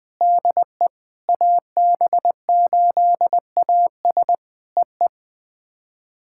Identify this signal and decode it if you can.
Morse code, heard as a keyed tone, about 20 wpm, 700 Hz. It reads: DE AB8AS EE
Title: POTA QSO Practice - 20wpm